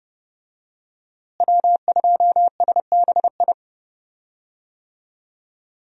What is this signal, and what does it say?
Morse code, heard as a keyed tone, about 29 wpm, 700 Hz. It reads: W2SBI
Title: USA Call Signs - 5 Characters - Repeated 3x - 30wpm